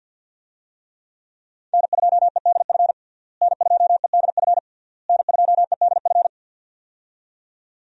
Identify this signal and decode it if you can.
Morse code, heard as a keyed tone, about 52 wpm, 700 Hz. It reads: N2EDF N2EDF N2EDF
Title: USA Call Signs - 5 Characters - Repeated 3x - 50wpm